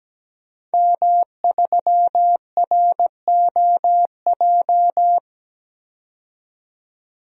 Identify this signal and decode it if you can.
Morse code, heard as a keyed tone, about 17 wpm, 700 Hz. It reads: M3ROJ
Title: DXCC-100 Call Signs - 5 Characters - 17wpm